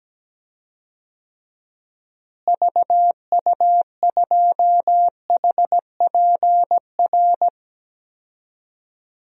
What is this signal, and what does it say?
Morse code, heard as a keyed tone, about 17 wpm, 700 Hz. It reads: VU2HPR
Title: DXCC-100 Call Signs - 6 Characters - Repeated 2x - 17wpm